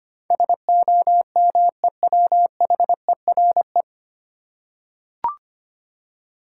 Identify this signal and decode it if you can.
Morse code, heard as a keyed tone, about 25 wpm, 690 Hz. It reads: SOMEWHERE
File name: practice / 25wpm / 9 Letter Words - First 3 Letters Given - 25wpm